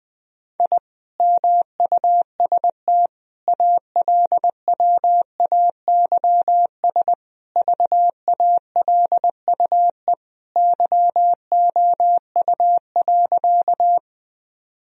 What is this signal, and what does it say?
Morse code, heard as a keyed tone, about 20 wpm, 700 Hz. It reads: I MUST ALWAYS VALUE YOU.
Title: Sentences from Top 1000 Words - Review Part 3 - 20wpm